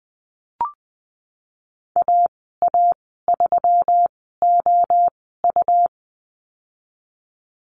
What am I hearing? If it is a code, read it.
Morse code, heard as a keyed tone, about 20 wpm, spaced as for 14 wpm, 700 Hz. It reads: AA3OU
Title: 147 Lesson 24g - Letter y - Call Signs 20wpm